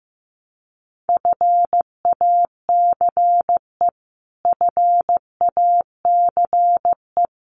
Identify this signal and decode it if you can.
Morse code, heard as a keyed tone, about 15 wpm, 700 Hz. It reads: FACE FACE
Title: Top 500 Words Repeated 15wpm